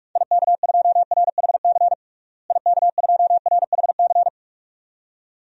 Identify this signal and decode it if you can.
Morse code, heard as a keyed tone, about 41 wpm, 700 Hz. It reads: IK2RHC IK2RHC
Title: DXCC-100 Call Signs - 6 Characters - Repeated 2x - 45wpm